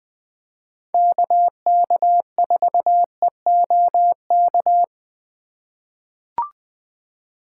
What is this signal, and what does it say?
Morse code, heard as a keyed tone, about 20 wpm, 700 Hz. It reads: KK4EOK
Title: USA Call Signs - 6 Characters - 4x Character Spacing - 20wpm